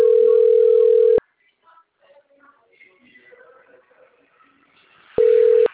Ringback tone (North America), 440 + 480 Hz.
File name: rg-1600-unknown- - - .48802